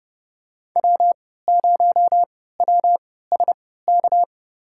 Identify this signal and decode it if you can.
Morse code, heard as a keyed tone, about 31 wpm, spaced as for 17 wpm, 700 Hz. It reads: W0WSK
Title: USA Call Signs - 5 Characters - 3x Character Spacing - 30wpm